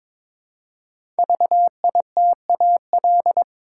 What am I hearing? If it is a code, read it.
Morse code, heard as a keyed tone, about 22 wpm, 700 Hz. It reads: VITAL
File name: 5 Letter Words - First 3 Letters Given - 22wpm